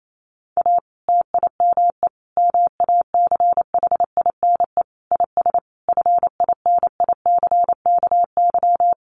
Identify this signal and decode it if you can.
Morse code, heard as a keyed tone, about 28 wpm, 700 Hz. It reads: A TIME MACHINE IS FINICKY
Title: A Sound Of Thunder 28wpm